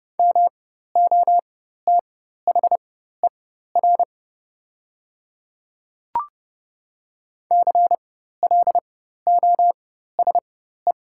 Morse code, heard as a keyed tone, about 30 wpm, spaced as for 14 wpm, 700 Hz. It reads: MOTHER CLOSE
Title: 4x Character Spacing - Top 500 Words - 30wpm